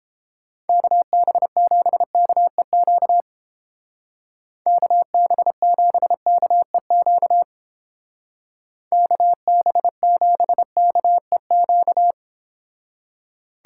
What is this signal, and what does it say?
Morse code, heard as a keyed tone, about 29 wpm, 700 Hz. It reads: KB7KEQ KB7KEQ KB7KEQ